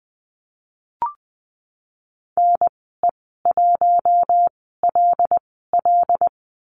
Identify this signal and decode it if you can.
Morse code, heard as a keyed tone, about 20 wpm, spaced as for 15 wpm, 700 Hz. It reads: NE1LL
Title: 082 Lesson 14g - Number 2 - Call Signs 20wpm